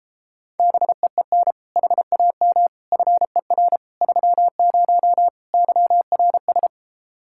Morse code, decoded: BEEN HAM FER 30 YRS

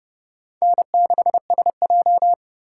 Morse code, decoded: N6SJ

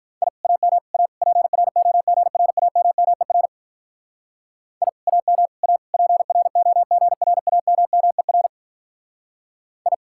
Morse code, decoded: I AM A PROGRAMMER I AM A PROGRAMMER I

